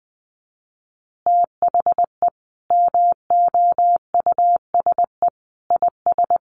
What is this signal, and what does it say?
Morse code, heard as a keyed tone, about 20 wpm, 700 Hz. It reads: THE MOUSE IS
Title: Easy Phrases 20wpm